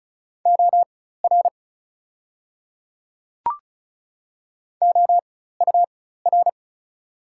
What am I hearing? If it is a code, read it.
Morse code, heard as a keyed tone, about 36 wpm, spaced as for 16 wpm, 690 Hz. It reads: OR OUR
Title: 4x Character Spacing - Top 200 Words - 35wpm